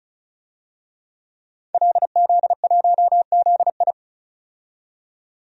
Morse code, heard as a keyed tone, about 34 wpm, 700 Hz. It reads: LZ1ZI
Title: DXCC-100 Call Signs - 5 Characters - 35wpm